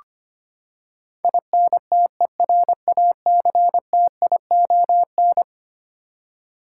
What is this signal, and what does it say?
Morse code, heard as a keyed tone, about 25 wpm, 700 Hz. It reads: INTERACTION
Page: morse-code-ninja-practice-25wpm